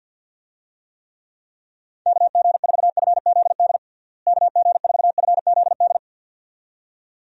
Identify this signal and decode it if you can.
Morse code, heard as a keyed tone, about 47 wpm, 700 Hz. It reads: KG4FZD KG4FZD
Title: USA Call Signs - 6 Characters - Repeated 2x - 50wpm